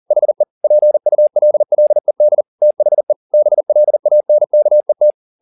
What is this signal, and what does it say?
Morse code, heard as a keyed tone, about 40 wpm, 600 Hz. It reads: HE PULLED THE BLANKET
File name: the old man and the sea 2 40wpm